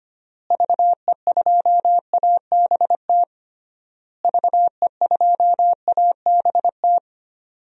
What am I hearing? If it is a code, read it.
Morse code, heard as a keyed tone, about 25 wpm, 700 Hz. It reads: VE2ABT VE2ABT
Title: DXCC-100 Call Signs - 6 Characters - Repeated 3x - 25wpm